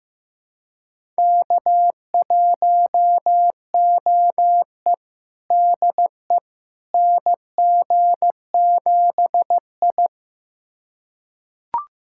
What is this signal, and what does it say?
Morse code, heard as a keyed tone, about 15 wpm, 700 Hz. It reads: K1OE DE NG7I
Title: USA Callsign DE Callsign - 4 Characters - 15wpm